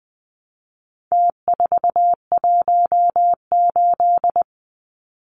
Morse code, heard as a keyed tone, about 20 wpm, 700 Hz. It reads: T418
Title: Numbers - Contest Style - 4-Digit 20wpm